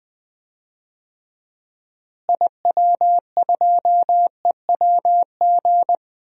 Morse code, heard as a keyed tone, about 20 wpm, 700 Hz. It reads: IW2EWG
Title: DXCC-100 Call Signs - 6 Characters - Repeated 3x - 20wpm